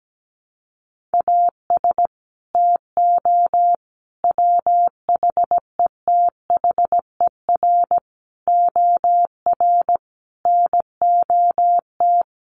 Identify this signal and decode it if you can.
Morse code, heard as a keyed tone, about 17 wpm, 700 Hz. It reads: AS TO WHETHER OR NOT
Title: Sets of 5 Words Encore 17wpm